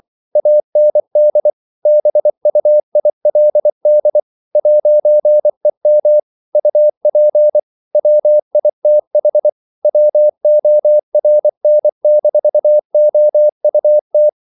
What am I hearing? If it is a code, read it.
Morse code, heard as a keyed tone, about 24 wpm, 600 Hz. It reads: AND BUILD 'EM UP WITH WORN-OUT
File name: CW / Practice / If24